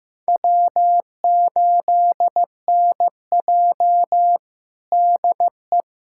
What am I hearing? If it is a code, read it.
Morse code, heard as a keyed tone, about 16 wpm, 700 Hz. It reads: W8NJ DE
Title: USA Callsign DE Callsign - 4 Characters - 15wpm